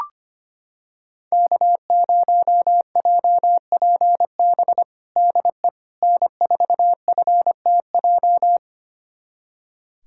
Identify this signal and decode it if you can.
Morse code, heard as a keyed tone, about 25 wpm, 700 Hz. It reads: K0JPB DE N4FTJ
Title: USA Callsign DE Callsign - 5 Characters - 25wpm